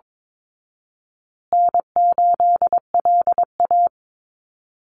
Morse code, heard as a keyed tone, about 22 wpm, 700 Hz. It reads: N8LA
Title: Easy Call Signs Repeated 2x - 22wpm